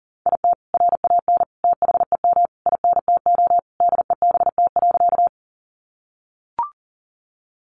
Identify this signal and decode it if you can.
Morse code, heard as a keyed tone, about 37 wpm, 700 Hz. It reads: IT RAN THEM INTO DEBT.
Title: Sentences from Top 1300 Words 40wpm